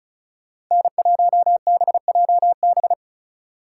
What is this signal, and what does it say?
Morse code, heard as a keyed tone, about 35 wpm, 700 Hz. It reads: N1BJB
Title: Speed Racing - USA 1x3 Call Signs 30wpm